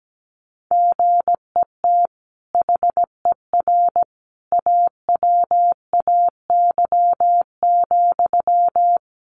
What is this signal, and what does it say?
Morse code, heard as a keyed tone, about 17 wpm, 700 Hz. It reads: GET HER AWAY,
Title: Sentences from Top 1500 Words 17wpm